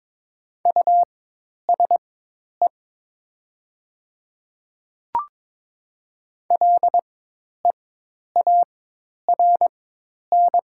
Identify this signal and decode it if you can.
Morse code, heard as a keyed tone, about 22 wpm, spaced as for 10 wpm, 700 Hz. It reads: USE LEARN
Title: 4x Character Spacing - Top 300 Words - 22wpm